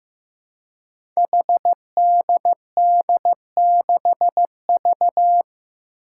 Morse code, heard as a keyed tone, about 15 wpm, 700 Hz. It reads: HDD6V
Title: Five Challenging Letter-Number Combinations 15wpm